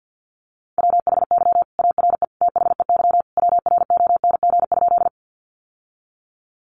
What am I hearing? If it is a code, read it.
Morse code, heard as a keyed tone, about 48 wpm, 700 Hz. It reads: WHY ARE THEY WRONG?
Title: Sentences from Top 900 Words 50wpm